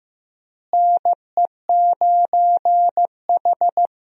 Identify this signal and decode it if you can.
Morse code, heard as a keyed tone, about 15 wpm, 700 Hz. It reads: NE9H
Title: USA Call Signs - 4 Characters - Repeated 3x - 15wpm